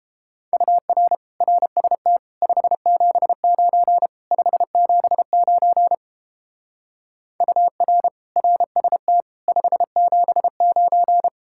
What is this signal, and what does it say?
Morse code, heard as a keyed tone, about 29 wpm, 700 Hz. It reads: UR RST 579 579 UR RST 579